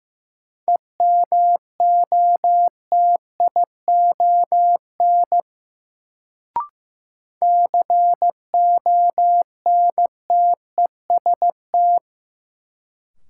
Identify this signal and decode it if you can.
Morse code, heard as a keyed tone, about 15 wpm, 700 Hz. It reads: EMOTION CONTEST
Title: Sentences from Top 1600 Words - New Word Drill - 15wpm